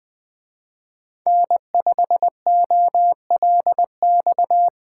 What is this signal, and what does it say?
Morse code, heard as a keyed tone, about 19 wpm, 700 Hz. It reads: N5OLX